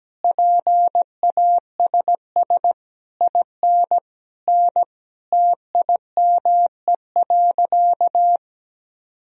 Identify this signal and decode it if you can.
Morse code, heard as a keyed tone, about 17 wpm, 700 Hz. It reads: PASS IN N TIME.
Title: Sentences from Top 400 Words and Abbreviations 17wpm